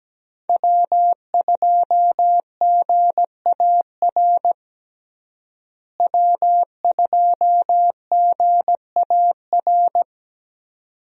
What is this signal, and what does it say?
Morse code, heard as a keyed tone, about 17 wpm, 700 Hz. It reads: W2GAR W2GAR
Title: USA Call Signs - 5 Characters - Repeated 2x - 17wpm